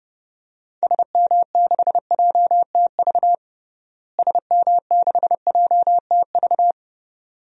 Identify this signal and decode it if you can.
Morse code, heard as a keyed tone, about 32 wpm, 700 Hz. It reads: SM6JTV SM6JTV
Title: DXCC-100 Call Signs - 6 Characters - Repeated 2x - 30wpm